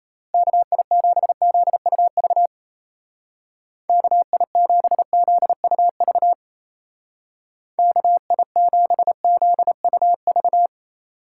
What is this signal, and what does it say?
Morse code, heard as a keyed tone, about 33 wpm, 700 Hz. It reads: KI7ZUV KI7ZUV KI7ZUV
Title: Speed Racing - USA 2x3 Call Signs 28wpm